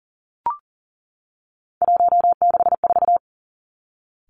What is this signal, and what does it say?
Morse code, heard as a keyed tone, about 40 wpm, 700 Hz. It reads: 164
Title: Three Challenging Letter-Number Combinations 40wpm